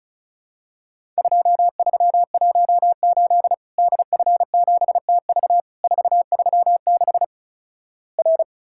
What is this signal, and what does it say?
Morse code, heard as a keyed tone, about 36 wpm, 700 Hz. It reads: 2318 DF7TV 436 R
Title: Worked All Europe DX Contest - QTC Exchange - 35wpm